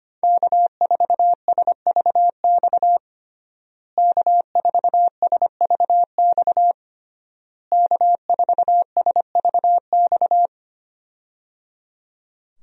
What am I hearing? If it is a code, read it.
Morse code, heard as a keyed tone, about 25 wpm, 700 Hz. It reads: K4SVX K4SVX K4SVX
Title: USA Call Signs - 5 Characters - Repeated 3x - 25wpm